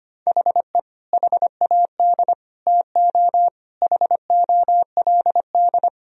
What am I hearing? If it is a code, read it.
Morse code, heard as a keyed tone, about 25 wpm, 700 Hz. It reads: HE HAD TO HOLD
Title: Sentences from Top 1000 Words - Review Part 2 - 25wpm